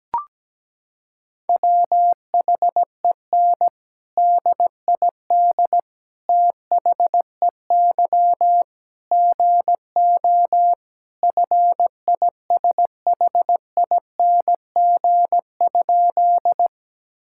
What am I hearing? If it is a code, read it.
Morse code, heard as a keyed tone, about 17 wpm, 700 Hz. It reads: WHEN DID THEY GO FISHING?
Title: Sentences from Top 1000 Words - Review Part 5 - 17wpm